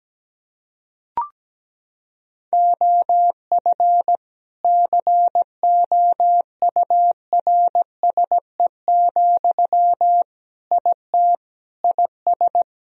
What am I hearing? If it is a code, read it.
Morse code, heard as a keyed tone, about 17 wpm, 700 Hz. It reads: OF COURSE, IT IS
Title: Sentences from Top 1000 Words - Review Part 2 - 17wpm